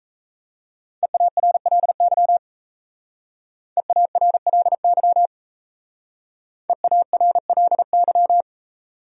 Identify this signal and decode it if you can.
Morse code, heard as a keyed tone, about 35 wpm, 700 Hz. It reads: EARLY EARLY EARLY
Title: Speed-Racing Top 500 Words 28wpm